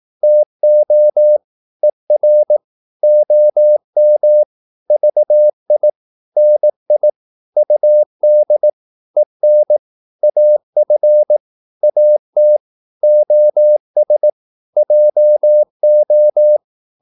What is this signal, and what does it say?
Morse code, heard as a keyed tone, about 18 wpm, 600 Hz. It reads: TO ER OM VI NI UD EN AF AT OS JO
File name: Korte ord DK 18wpm